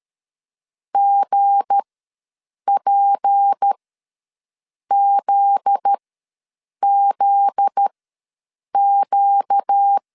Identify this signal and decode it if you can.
Morse code, heard as a keyed tone, about 13 wpm, spaced as for 7 wpm, 800 Hz. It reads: GPZZQ
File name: 7 GF9Z ws 12wpm